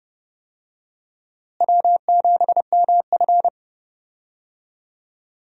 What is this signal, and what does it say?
Morse code, heard as a keyed tone, about 30 wpm, 700 Hz. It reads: W7MF
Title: USA Call Signs - 4 Characters - Repeated 2x - 30wpm